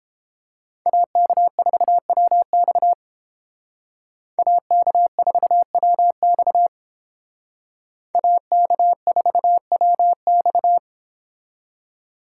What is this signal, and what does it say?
Morse code, heard as a keyed tone, about 31 wpm, 700 Hz. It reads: AK4WX AK4WX AK4WX